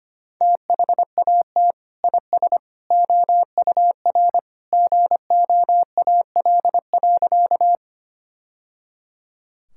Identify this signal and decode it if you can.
Morse code, heard as a keyed tone, about 25 wpm, 700 Hz. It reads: THAT IS OUR GOAL.
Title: Sentences from Top 1000 Words - Review Part 5 - 25wpm